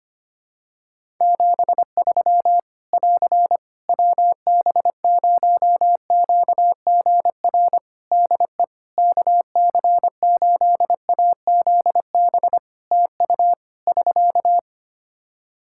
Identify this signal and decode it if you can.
Morse code, heard as a keyed tone, about 25 wpm, 700 Hz. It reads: 73 <AR> WB0QGR DE KC8AZB TU <SK>
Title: Practice QSO - Exchange 4 - Ending - 25wpm